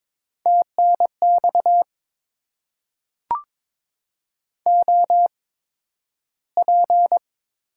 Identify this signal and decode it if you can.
Morse code, heard as a keyed tone, about 22 wpm, spaced as for 12 wpm, 700 Hz. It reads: TNX OP
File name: Top 90 QSO Elements - 8x Character Spacing - 22wpm